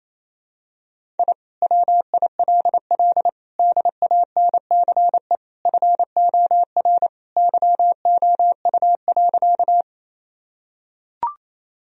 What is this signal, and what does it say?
Morse code, heard as a keyed tone, about 28 wpm, 700 Hz. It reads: I WILL DANCE FOR YOU.
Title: Sentences from Top 1000 Words - Review Part 4 - 28wpm